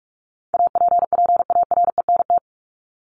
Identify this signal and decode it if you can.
Morse code, heard as a keyed tone, about 44 wpm, 700 Hz. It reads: APPARENT
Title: LY Suffix - Top 100 Words with Base Words 45wpm